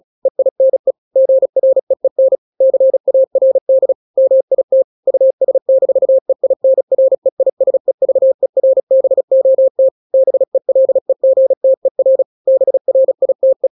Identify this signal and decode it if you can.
Morse code, heard as a keyed tone, about 36 wpm, 530 Hz. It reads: EINE GREEN CARD MIT US-EINREISEVERBOT BELEGTER BRITE
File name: daily / 35wpm